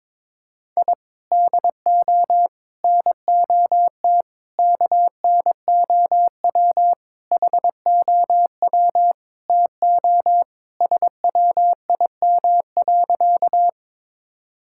Morse code, heard as a keyed tone, about 22 wpm, 700 Hz. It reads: I DO NOT KNOW HOW TO SWIM.
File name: practice / 22wpm / Sentences with 4-Word Phrases - 22wpm